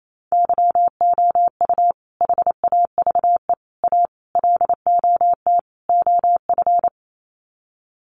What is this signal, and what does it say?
Morse code, heard as a keyed tone, about 28 wpm, 690 Hz. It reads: YOU HAVE A LOT OF
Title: Sets of 5 Words Encore 28wpm